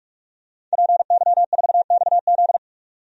Morse code, heard as a keyed tone, about 41 wpm, 700 Hz. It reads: PY4XZ
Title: DXCC-100 Call Signs - 5 Characters - 45wpm